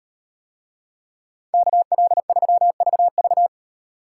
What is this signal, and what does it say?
Morse code, heard as a keyed tone, about 36 wpm, 700 Hz. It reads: KL3VV